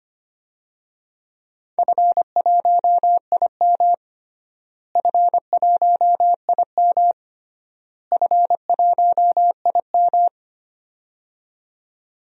Morse code, heard as a keyed tone, about 25 wpm, 700 Hz. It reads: F1IM F1IM F1IM
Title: DXCC-100 Call Signs - 4 Characters - Repeated 3x - 25wpm